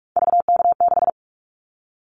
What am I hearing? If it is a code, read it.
Morse code, heard as a keyed tone, about 46 wpm, 700 Hz. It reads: VK6
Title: Callsign Prefixes for Top 50 DXCC - 45wpm